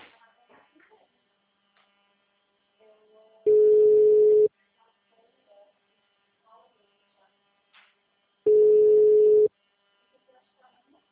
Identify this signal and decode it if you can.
Ringback tone (Europe, 425 Hz), with speech.